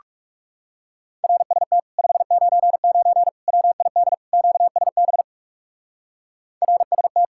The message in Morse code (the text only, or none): RST 599 WID QSB RST